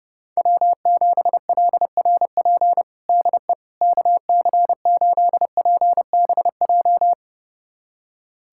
Morse code, heard as a keyed tone, about 30 wpm, 700 Hz. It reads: W7LRP DE KC8PBJ
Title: USA Callsign DE Callsign 30wpm